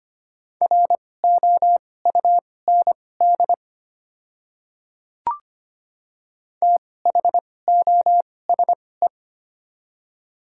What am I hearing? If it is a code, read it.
Morse code, heard as a keyed tone, about 25 wpm, spaced as for 18 wpm, 700 Hz. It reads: ROUND THOSE
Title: 2x Character Spacing - Top 300 Words - 25wpm